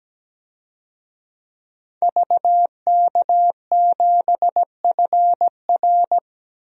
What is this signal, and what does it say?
Morse code, heard as a keyed tone, about 17 wpm, 700 Hz. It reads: VK7FR
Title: DXCC-100 Call Signs - 5 Characters - 17wpm